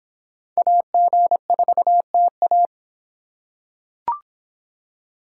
Morse code, heard as a keyed tone, about 26 wpm, 700 Hz. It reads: AG4TA